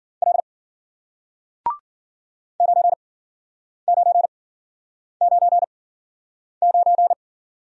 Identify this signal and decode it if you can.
Morse code, heard as a keyed tone, about 54 wpm, spaced as for 9 wpm, 700 Hz. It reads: L 9999